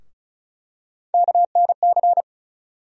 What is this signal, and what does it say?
Morse code, heard as a keyed tone, about 36 wpm, 700 Hz. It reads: KNC
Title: USA Call Sign Postfixes - 3 Letters 35wpm